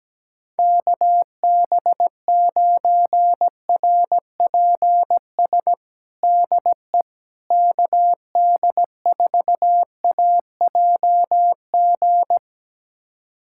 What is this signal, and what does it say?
Morse code, heard as a keyed tone, about 17 wpm, 700 Hz. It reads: KB9RPS DE KD4AJG
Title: USA Callsign DE Callsign - 6 Characters - 17wpm